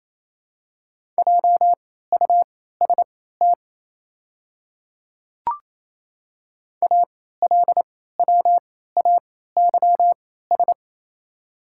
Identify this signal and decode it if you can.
Morse code, heard as a keyed tone, about 28 wpm, spaced as for 16 wpm, 700 Hz. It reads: JUST ALWAYS
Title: 3x Character Spacing - Top 300 Words - 28wpm